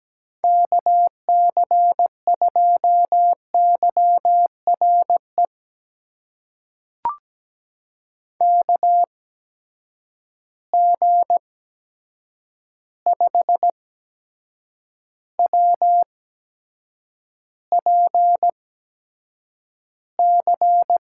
Morse code, heard as a keyed tone, about 17 wpm, spaced as for 7 wpm, 700 Hz. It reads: KC2YRE KG5WPC
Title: USA Call Signs - 6 Characters - 8x Character Spacing - 17wpm